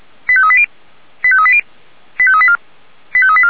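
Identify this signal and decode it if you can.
Five-tone selective call (ZVEI), heard as quick runs of five tones, tones of about 70 ms. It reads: 85380 85380 85384 85384
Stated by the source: Five-tone ZVEI selective call, code 89335, received by SDR.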